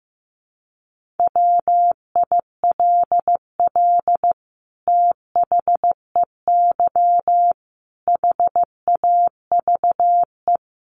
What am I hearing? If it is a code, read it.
Morse code, heard as a keyed tone, about 15 wpm, 700 Hz. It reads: WILL THEY HAVE
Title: Sentences from Top 1500 Words 15wpm